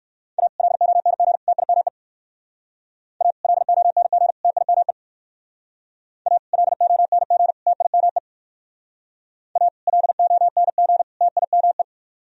Morse code, heard as a keyed tone, about 62 wpm, 700 Hz. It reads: A LONG TIME A LONG TIME A LONG TIME A LONG TIME